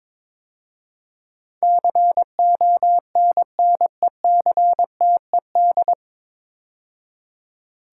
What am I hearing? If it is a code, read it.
Morse code, heard as a keyed tone, about 22 wpm, 700 Hz. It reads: CONNECTED
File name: 9 Letter Words - First 3 Letters Given - 22wpm